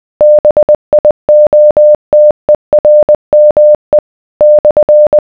Morse code, heard as a keyed tone, about 20 wpm, 600 Hz. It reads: BIOTERME /